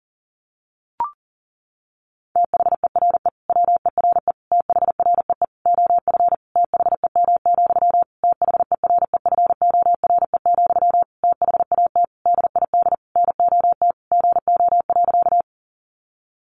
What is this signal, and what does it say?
Morse code, heard as a keyed tone, about 40 wpm, 700 Hz. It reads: THERE WERE THREE OF THEM, THEREFORE, THAT DID NOT GO.
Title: Sentences from Top 800 Words 40wpm